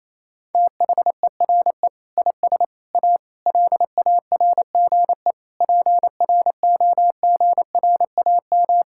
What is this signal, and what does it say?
Morse code, heard as a keyed tone, about 28 wpm, 700 Hz. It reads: THERE IS A LARGE PROGRAM
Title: Sentences from Top 1000 Words - Review Part 5 - 28wpm